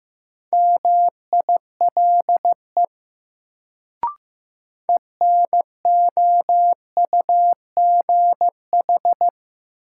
Morse code, heard as a keyed tone, about 16 wpm, 700 Hz. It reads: MILE ENOUGH
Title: Sentences from Top 400 Words - New Word Drill - 15wpm